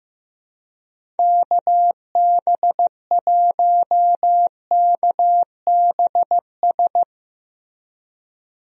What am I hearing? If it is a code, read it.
Morse code, heard as a keyed tone, about 15 wpm, 700 Hz. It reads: KB1KBS
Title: USA Call Signs - 6 Characters - Repeated 3x - 15wpm